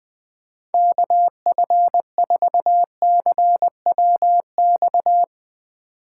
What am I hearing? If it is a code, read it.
Morse code, heard as a keyed tone, about 20 wpm, 700 Hz. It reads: KF4CWX